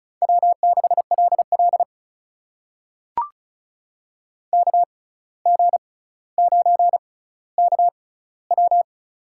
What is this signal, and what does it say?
Morse code, heard as a keyed tone, about 36 wpm, spaced as for 17 wpm, 700 Hz. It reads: W6LL KG9KW
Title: USA Call Signs - 6x Character Spacing - 35wpm